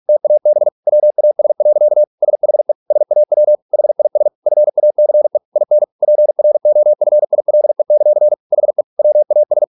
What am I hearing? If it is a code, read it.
Morse code, heard as a keyed tone, about 46 wpm, 600 Hz. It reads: TAB WAS. SHE SAW HIS FACE IN PROFILE; HE WAS